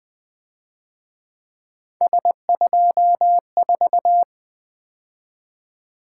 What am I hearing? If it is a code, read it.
Morse code, heard as a keyed tone, about 24 wpm, 700 Hz. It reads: S24
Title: Speed-Racing - Three Letter-Number Combinations - 20wpm